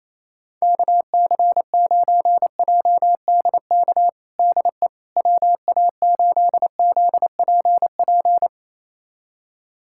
Morse code, heard as a keyed tone, about 27 wpm, 700 Hz. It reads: KC9JDK DE WA8ZPP
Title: USA Callsign DE Callsign - 6 Characters - 28wpm